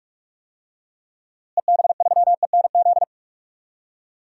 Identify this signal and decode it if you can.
Morse code, heard as a keyed tone, about 43 wpm, 700 Hz. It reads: EB3ENZ
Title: DXCC-100 Call Signs - 6 Characters - Repeated 2x - 45wpm